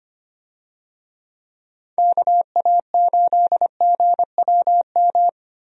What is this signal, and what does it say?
Morse code, heard as a keyed tone, about 25 wpm, 700 Hz. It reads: KA8GWM
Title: USA Call Signs - 6 Characters - Repeated 2x - 25wpm